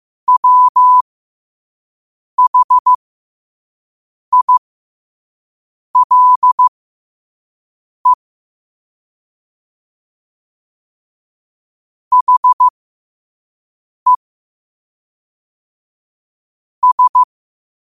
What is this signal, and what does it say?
Morse code, heard as a keyed tone, about 15 wpm, spaced as for 5 wpm, 1000 Hz. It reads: WHILE HE S
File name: qotdfiles / 5wpm